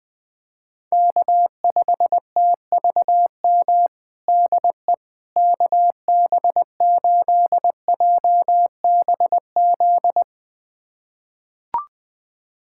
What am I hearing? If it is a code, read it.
Morse code, heard as a keyed tone, about 20 wpm, 700 Hz. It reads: K5TVM DE KB8JBZ
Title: USA Callsign DE Callsign 20wpm